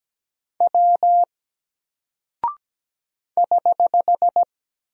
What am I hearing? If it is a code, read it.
Morse code, heard as a keyed tone, about 17 wpm, 700 Hz. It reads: W <HH>
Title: Single Character (Letter:Number:Punctuation:Prosign) Alt 17wpm